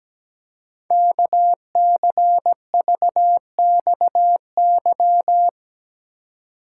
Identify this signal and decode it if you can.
Morse code, heard as a keyed tone, about 17 wpm, 700 Hz. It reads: KCVXY
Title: Code Groups - 5 Letters - 17wpm